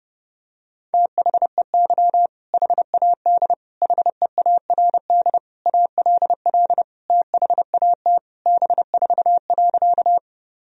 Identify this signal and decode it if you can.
Morse code, heard as a keyed tone, about 30 wpm, 700 Hz. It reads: THEY HAD HEARD ALL THAT B4.
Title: Sentences from Top 400 Words and Abbreviations 30wpm